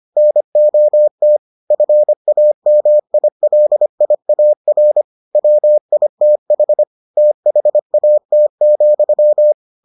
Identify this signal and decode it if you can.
Morse code, heard as a keyed tone, about 25 wpm, 600 Hz. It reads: NOT FAMILIAR WITH THAT,